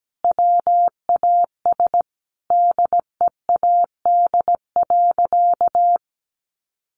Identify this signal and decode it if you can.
Morse code, heard as a keyed tone, about 17 wpm, 700 Hz. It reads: WAS DEAD.
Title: Sentences from Top 1500 Words 17wpm